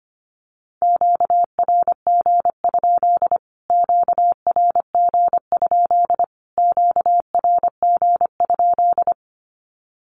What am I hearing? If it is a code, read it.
Morse code, heard as a keyed tone, about 25 wpm, 700 Hz. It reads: QRG? QRG? QRG?
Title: Q-Codes Repeated 3x 25wpm